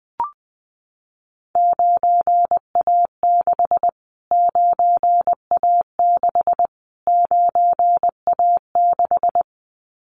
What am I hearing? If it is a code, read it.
Morse code, heard as a keyed tone, about 19 wpm, 700 Hz. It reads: 9A6 9A6 9A6
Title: Callsign Prefixes for Top 25 DXCC - Repeated 3x - 20wpm